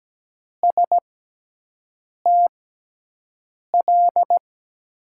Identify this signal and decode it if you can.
Morse code, heard as a keyed tone, about 17 wpm, spaced as for 6 wpm, 700 Hz. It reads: STL
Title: USA Call Signs - 6 Characters - 6x Character Spacing - 17wpm